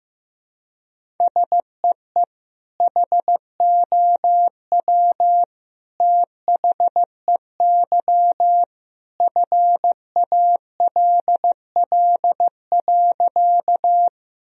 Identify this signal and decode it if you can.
Morse code, heard as a keyed tone, about 15 wpm, 700 Hz. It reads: SEE HOW THEY FALL.
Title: Sentences from Top 1000 Words - Review Part 2 - 15wpm